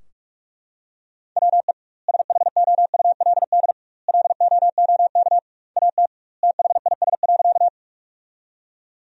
Morse code, heard as a keyed tone, about 42 wpm, 700 Hz. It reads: WE SHOULD LOOK AT THIS.
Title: 2x Word-Spacing - Sentences from Top 300 Words - 45wpm